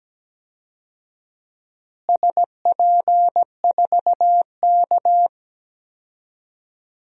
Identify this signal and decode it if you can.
Morse code, heard as a keyed tone, about 17 wpm, 700 Hz. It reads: SP4K
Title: DXCC-100 Call Signs - 4 Characters - Repeated 3x - 17wpm